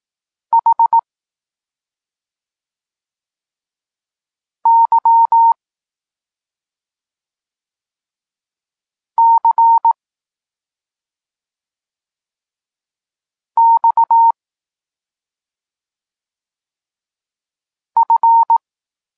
Morse code, heard as a keyed tone, about 18 wpm, 920 Hz. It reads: H Y C X F
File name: Tekst-34 4